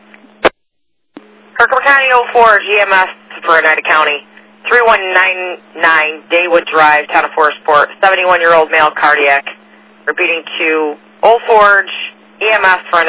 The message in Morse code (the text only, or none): none